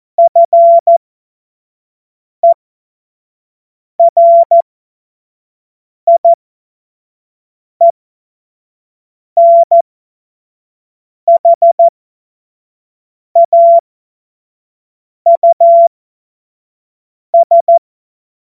Morse code, heard as a keyed tone, about 14 wpm, spaced as for 5 wpm, 680 Hz. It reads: FERIENHAUS
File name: Morsekurs / Witze / Witz 15. Februar 2022 25 F70